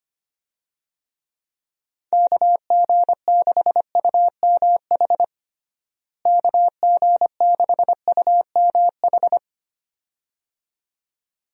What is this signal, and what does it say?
Morse code, heard as a keyed tone, about 25 wpm, 700 Hz. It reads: KG6UMH KG6UMH
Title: USA Call Signs - 6 Characters - Repeated 2x - 25wpm